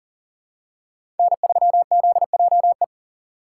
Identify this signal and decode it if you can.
Morse code, heard as a keyed tone, about 39 wpm, 700 Hz. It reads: N3ZJE
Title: USA Call Signs - 5 Characters - 6x Character Spacing - 40wpm